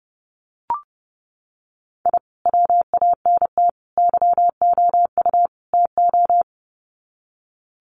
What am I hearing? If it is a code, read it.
Morse code, heard as a keyed tone, about 30 wpm, 700 Hz. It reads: I WANT YOU TO
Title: Sets of 4 Words 30wpm